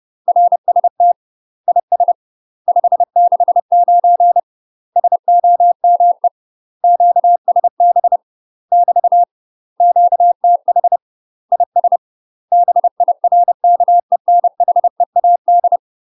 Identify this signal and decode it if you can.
Morse code, heard as a keyed tone, about 30 wpm, 700 Hz. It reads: RST IS 569 SOME QSB <BT> QTH IS BIRKENHEAD
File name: UK MASTER QSOs TRAINING / 30WPM / UK MASTER QSOs 0029